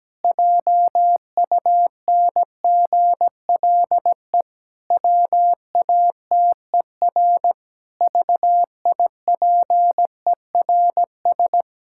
Morse code, heard as a keyed tone, about 17 wpm, 700 Hz. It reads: JUNGLE WATER VIPERS
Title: Pangram Challenge v2 - 17wpm